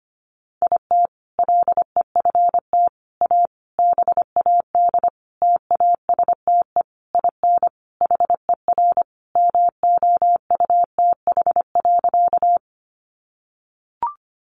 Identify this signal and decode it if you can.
Morse code, heard as a keyed tone, about 25 wpm, 700 Hz. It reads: IT LEFT A BAD TASTE IN HER MOUTH.
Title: Sentences from Top 1500 Words 25wpm